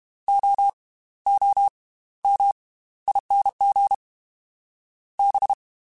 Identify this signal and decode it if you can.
Morse code, heard as a keyed tone, about 32 wpm, spaced as for 17 wpm, 800 Hz. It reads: OOMING B
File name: Slow ing @30